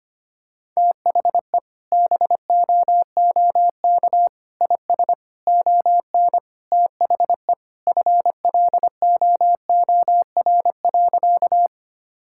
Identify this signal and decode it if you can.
Morse code, heard as a keyed tone, about 25 wpm, 700 Hz. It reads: THE BOOK IS ON THE FLOOR.
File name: practice / 25wpm / Sentences with 3-Word Phrases - 25wpm